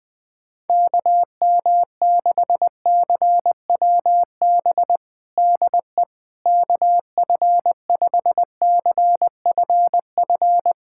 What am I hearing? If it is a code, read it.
Morse code, heard as a keyed tone, about 21 wpm, 700 Hz. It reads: KM6CWB DE KF5CFF
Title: USA Callsign DE Callsign 20wpm